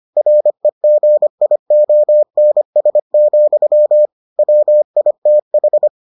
Morse code, heard as a keyed tone, about 25 wpm, 600 Hz. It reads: REGIONS, WITH